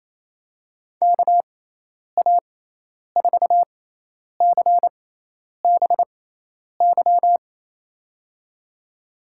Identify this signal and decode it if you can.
Morse code, heard as a keyed tone, about 27 wpm, spaced as for 10 wpm, 700 Hz. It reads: KA4CBY
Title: USA Call Signs - 6 Characters - 6x Character Spacing - 28wpm